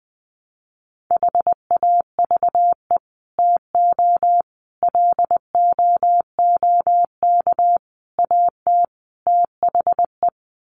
Morse code, heard as a keyed tone, about 20 wpm, 700 Hz. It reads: HAVE TO LOOK AT THE
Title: Sets of 5 Words Encore 20wpm